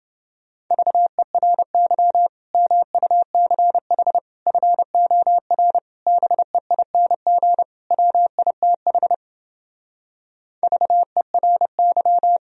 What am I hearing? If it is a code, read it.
Morse code, heard as a keyed tone, about 32 wpm, 700 Hz. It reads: VERY MUCH FOR BEING WITH VERY